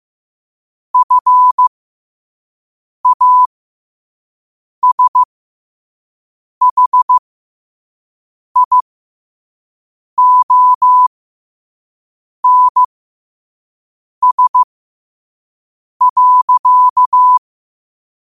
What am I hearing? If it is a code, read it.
Morse code, heard as a keyed tone, about 15 wpm, spaced as for 5 wpm, 1000 Hz. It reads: FASHIONS.
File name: qotdfiles / 5wpm